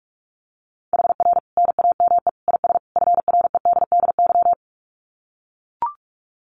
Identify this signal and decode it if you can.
Morse code, heard as a keyed tone, about 44 wpm, 700 Hz. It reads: HR NAME IS FREDDY
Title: Common QSO Phrases 45wpm